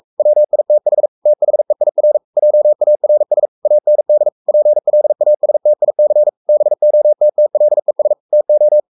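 Morse code, heard as a keyed tone, about 44 wpm, 600 Hz. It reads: WITH THEIR JARS AND PLASTIC BOTTLES TO
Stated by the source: Morse code at 43 wpm sending Ken Follett's novel Never in English, from CW med Gnister.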